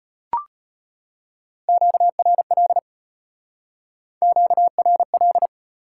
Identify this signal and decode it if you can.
Morse code, heard as a keyed tone, about 32 wpm, 690 Hz. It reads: QRL QRL